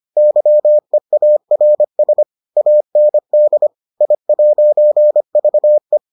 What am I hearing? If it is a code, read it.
Morse code, heard as a keyed tone, about 25 wpm, 600 Hz. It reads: YEARS AND I'VE